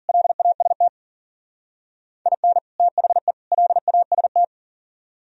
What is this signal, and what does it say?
Morse code, heard as a keyed tone, about 50 wpm, 700 Hz. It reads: LAST IN THE LAST